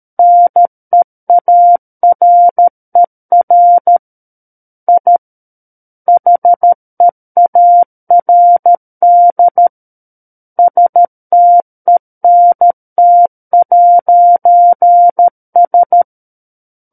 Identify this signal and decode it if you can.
Morse code, heard as a keyed tone, about 13 wpm, 700 Hz. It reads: NEARER I HEARD STENT'S
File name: Book One / 04-Chapter 4 - 13 WPM